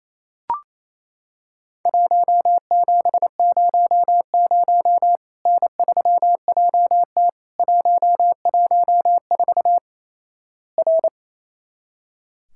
Morse code, heard as a keyed tone, about 28 wpm, 700 Hz. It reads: 1700 N3JT 114 R
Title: Worked All Europe DX Contest - QTC Exchange - 28wpm